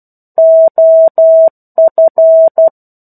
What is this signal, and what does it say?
Morse code, heard as a keyed tone, about 12 wpm, 650 Hz. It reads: OF
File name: 01-Chapter-01 12 WPM